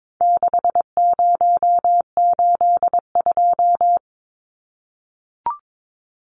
Morse code, heard as a keyed tone, about 22 wpm, 700 Hz. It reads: 6082
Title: Numbers - 4-Digit 22wpm